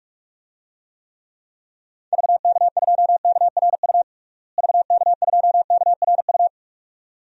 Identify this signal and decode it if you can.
Morse code, heard as a keyed tone, about 41 wpm, 700 Hz. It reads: VK2KRU VK2KRU
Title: DXCC-100 Call Signs - 6 Characters - Repeated 2x - 45wpm